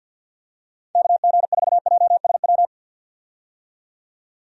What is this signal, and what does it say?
Morse code, heard as a keyed tone, about 51 wpm, 700 Hz. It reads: KG4JSW
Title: USA Call Signs - 6 Characters - 50wpm